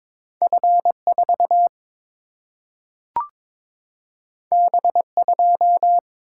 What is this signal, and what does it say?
Morse code, heard as a keyed tone, about 22 wpm, 700 Hz. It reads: F4 B2
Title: Single-Letter Single-Number 22wpm